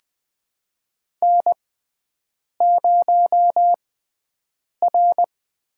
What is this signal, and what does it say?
Morse code, heard as a keyed tone, about 20 wpm, spaced as for 7 wpm, 700 Hz. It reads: N0R
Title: USA Call Signs - 6x Character Spacing - 20wpm